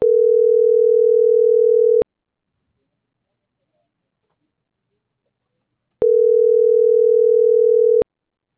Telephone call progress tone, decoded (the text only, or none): ringback tone